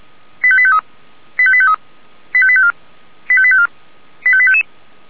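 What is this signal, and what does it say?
Five-tone selective call (ZVEI), heard as quick runs of five tones, tones of about 70 ms. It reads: 85853 85853 85854 85854 85855
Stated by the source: ZVEI five-tone selective call received by SDR, address 85738.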